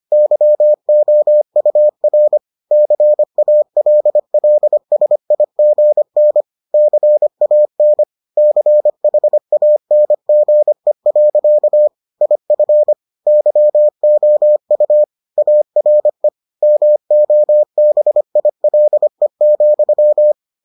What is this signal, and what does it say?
Morse code, heard as a keyed tone, about 25 wpm, 600 Hz. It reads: YOUR CALLSIGN CAN CHANGE. IF YOU ARE MOBILE,